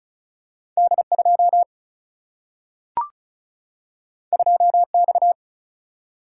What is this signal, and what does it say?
Morse code, heard as a keyed tone, about 36 wpm, 700 Hz. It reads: D2 2X
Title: Single-Letter Single-Number 35wpm